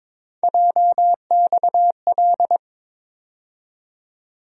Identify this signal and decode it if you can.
Morse code, heard as a keyed tone, about 22 wpm, 700 Hz. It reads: JXL
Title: USA Call Sign Postfixes - 3 Letters 22wpm